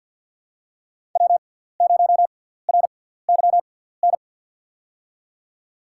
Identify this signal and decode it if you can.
Morse code, heard as a keyed tone, about 51 wpm, spaced as for 17 wpm, 700 Hz. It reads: W0RYN
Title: USA Call Signs - 5 Characters - 6x Character Spacing - 50wpm